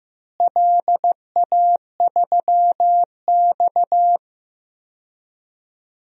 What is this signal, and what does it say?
Morse code, heard as a keyed tone, about 15 wpm, 700 Hz. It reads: LA3X
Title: DXCC-100 Call Signs - 4 Characters - Repeated 2x - 15wpm